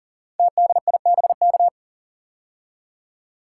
Morse code, heard as a keyed tone, about 41 wpm, 690 Hz. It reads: TDIBK
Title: Code Groups - 5 Letters - 40wpm